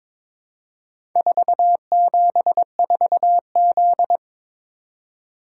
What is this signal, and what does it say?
Morse code, heard as a keyed tone, about 22 wpm, 700 Hz. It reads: 474Z
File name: practice / 22wpm / Four Letter-Number Combinations - 22wpm